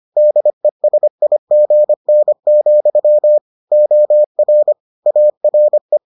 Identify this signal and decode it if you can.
Morse code, heard as a keyed tone, about 25 wpm, 600 Hz. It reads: DESIGN, OR ARE